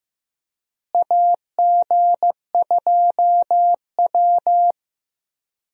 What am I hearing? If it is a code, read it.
Morse code, heard as a keyed tone, about 15 wpm, 700 Hz. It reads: AG2W
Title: USA Call Signs - 4 Characters - Repeated 3x - 15wpm